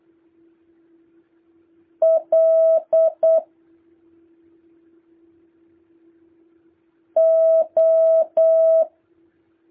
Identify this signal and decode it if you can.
Morse code, heard as a keyed tone, about 8 wpm, spaced as for 2 wpm, 650 Hz. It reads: LO